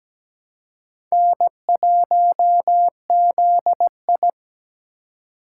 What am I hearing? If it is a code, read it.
Morse code, heard as a keyed tone, about 17 wpm, 700 Hz. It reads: N1ZI
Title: USA Call Signs - 4 Characters - 2x Character Spacing - 17wpm